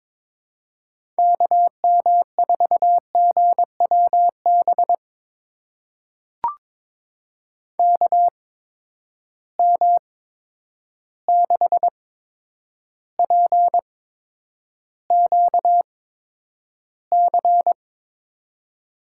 Morse code, heard as a keyed tone, about 22 wpm, spaced as for 9 wpm, 700 Hz. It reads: KM4GWB KM6PQC
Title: USA Call Signs - 6 Characters - 8x Character Spacing - 22wpm